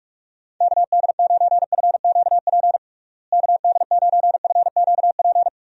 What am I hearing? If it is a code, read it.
Morse code, heard as a keyed tone, about 46 wpm, 700 Hz. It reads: KD9FQP KD9FQP
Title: USA Call Signs - 6 Characters - Repeated 2x - 45wpm